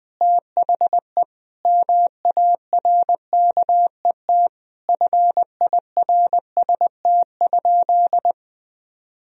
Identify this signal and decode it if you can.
Morse code, heard as a keyed tone, about 21 wpm, 700 Hz. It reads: THE MARKET FIRST?
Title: Sentences from Top 1000 Words - Review Part 3 - 20wpm